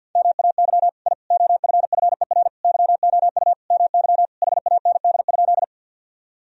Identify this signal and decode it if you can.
Morse code, heard as a keyed tone, about 46 wpm, 700 Hz. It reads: MAY I OFFER YOU MY HAND?
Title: Sentences from Top 1000 Words - Review Part 4 - 50wpm